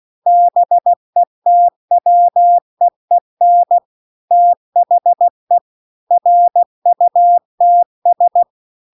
Morse code, heard as a keyed tone, about 16 wpm, 700 Hz. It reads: BETWEEN THE RUTS